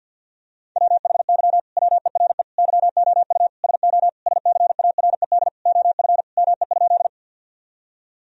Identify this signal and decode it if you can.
Morse code, heard as a keyed tone, about 50 wpm, 700 Hz. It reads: WHY WERE YOU SO SCARED OF ME?
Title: Sentences from Top 1000 Words - Review Part 4 - 50wpm